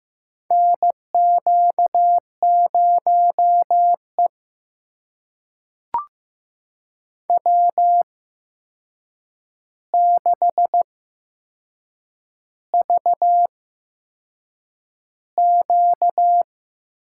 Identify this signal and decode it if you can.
Morse code, heard as a keyed tone, about 15 wpm, spaced as for 6 wpm, 700 Hz. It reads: NQ0E W6VQ
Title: USA Call Signs - 4 Characters - 8x Character Spacing - 15wpm